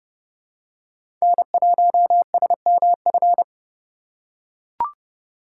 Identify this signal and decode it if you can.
Morse code, heard as a keyed tone, about 32 wpm, 700 Hz. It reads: N1SMF
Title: USA Call Signs - 5 Characters - 6x Character Spacing - 30wpm